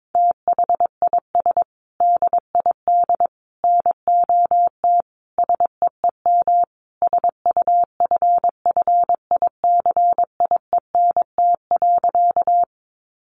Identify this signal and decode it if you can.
Morse code, heard as a keyed tone, about 22 wpm, 700 Hz. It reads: THIS DID NOT SEEM SUFFICIENT.
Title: Sentences from Top 1400 Words 22wpm